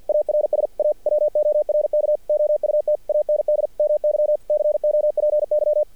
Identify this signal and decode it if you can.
Morse code, heard as a keyed tone, about 50 wpm, 600 Hz. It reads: ARS A WORK OUT AND MY COPY
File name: archived / CRSnet